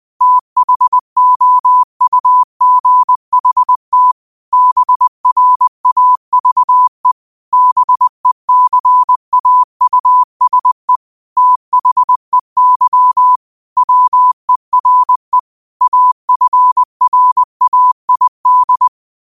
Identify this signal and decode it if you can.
Morse code, heard as a keyed tone, about 20 wpm, 1000 Hz. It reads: THOUGHT BRAVE BECAUSE THEY WERE AFRAID
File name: qotdfiles / 20wpm